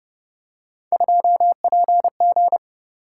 Morse code, heard as a keyed tone, about 29 wpm, 700 Hz. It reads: 2PG